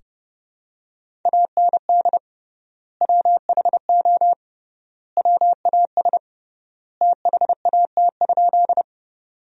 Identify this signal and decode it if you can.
Morse code, heard as a keyed tone, about 30 wpm, 700 Hz. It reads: AND WHO WAS THAT?
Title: 3x Word-Spacing - Sentences from Top 100 Words - 30wpm